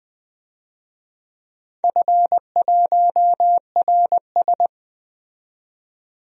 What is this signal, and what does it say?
Morse code, heard as a keyed tone, about 20 wpm, 700 Hz. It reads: F1RS
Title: DXCC-100 Call Signs - 4 Characters - 20wpm